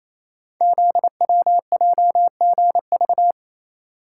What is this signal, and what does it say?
Morse code, heard as a keyed tone, about 29 wpm, 700 Hz. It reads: ZWJGV
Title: Code Groups - 5 Letters - 28wpm